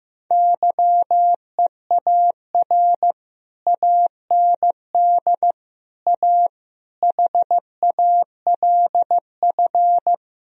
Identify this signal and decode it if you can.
Morse code, heard as a keyed tone, about 15 wpm, 700 Hz. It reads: YEAR AND A HALF
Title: Speed-Racing - Sets of 4 Words 15wpm